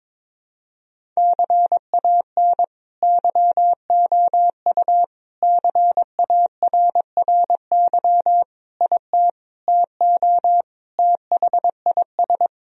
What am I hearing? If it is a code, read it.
Morse code, heard as a keyed tone, about 22 wpm, 700 Hz. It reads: CAN YOU CARRY IT TO THIS
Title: Sentences from Top 1000 Words - Review Part 1 - 22wpm